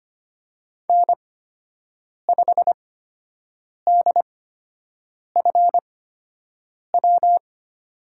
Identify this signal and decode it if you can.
Morse code, heard as a keyed tone, about 25 wpm, spaced as for 7 wpm, 700 Hz. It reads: N5DFW
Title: USA Call Signs - 5 Characters - 8x Character Spacing - 25wpm